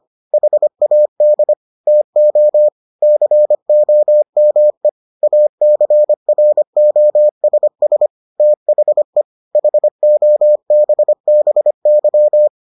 Morse code, heard as a keyed tone, about 25 wpm, 600 Hz. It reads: HAD TO COME ACROSS THE HOBBY